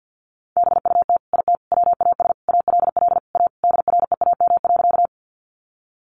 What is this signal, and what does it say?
Morse code, heard as a keyed tone, about 50 wpm, 700 Hz. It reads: BUT IT WAS ALL A DREAM.
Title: Sentences from Top 1500 Words 50wpm